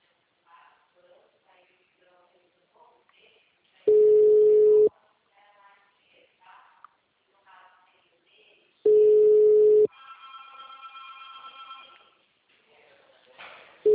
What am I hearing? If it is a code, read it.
Ringback tone (Europe, 425 Hz), with speech.
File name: out- -2029- - - .15919